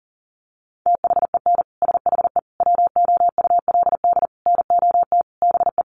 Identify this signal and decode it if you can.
Morse code, heard as a keyed tone, about 40 wpm, 700 Hz. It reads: THEN SHE WOULD NOT BE
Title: Sentences from Top 1400 Words 40wpm